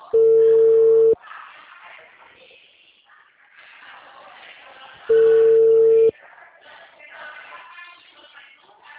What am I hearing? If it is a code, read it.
Ringback tone (China, 450 Hz).